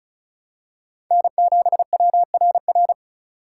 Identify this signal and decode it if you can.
Morse code, heard as a keyed tone, about 34 wpm, 700 Hz. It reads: N7WRR
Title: USA Call Signs - 5 Characters - 4x Character Spacing - 35wpm